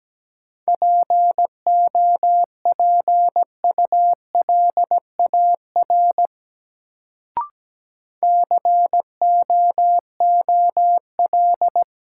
Morse code, heard as a keyed tone, about 17 wpm, 700 Hz. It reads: POPULAR COOL
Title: Sentences from Top 1200 Words - New Word Drill Alt - 17wpm